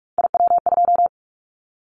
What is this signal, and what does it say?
Morse code, heard as a keyed tone, about 46 wpm, 700 Hz. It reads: IW2
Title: Callsign Prefixes for Top 25 DXCC - 45wpm